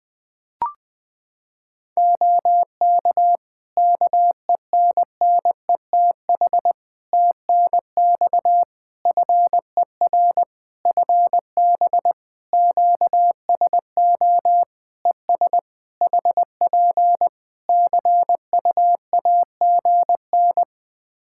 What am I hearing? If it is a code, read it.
Morse code, heard as a keyed tone, about 20 wpm, 700 Hz. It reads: OK KENNETH TNX FER FB QSO ES HP CUAGN
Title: Practice QSO - Exchange 4 - Ending - 20wpm